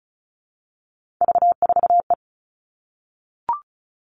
Morse code, heard as a keyed tone, about 36 wpm, 700 Hz. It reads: V4E
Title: Three Challenging Letter-Number Combinations 35wpm